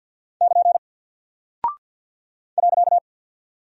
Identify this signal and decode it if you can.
Morse code, heard as a keyed tone, about 48 wpm, 700 Hz. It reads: <KN> .
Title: Single Character (Letter:Number:Punctuation:Prosign) Alt 50wpm